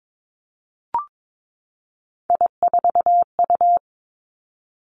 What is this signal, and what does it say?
Morse code, heard as a keyed tone, about 22 wpm, 700 Hz. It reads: I4U
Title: Three Challenging Letter-Number Combinations 22wpm